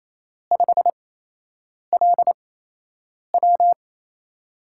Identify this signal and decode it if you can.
Morse code, heard as a keyed tone, about 28 wpm, spaced as for 8 wpm, 700 Hz. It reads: 5LW
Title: USA Call Signs - 8x Character Spacing - 28wpm